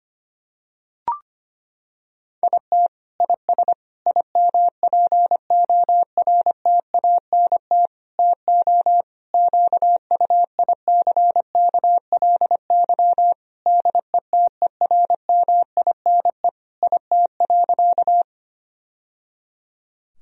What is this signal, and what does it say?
Morse code, heard as a keyed tone, about 25 wpm, 700 Hz. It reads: IT IS IMPORTANT TO QUICKLY DETERMINE IT.
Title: Sentences from Top 1000 Words - Review Part 4 - 25wpm